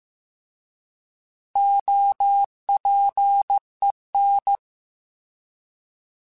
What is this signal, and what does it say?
Morse code, heard as a keyed tone, about 15 wpm, 800 Hz. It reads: OPEN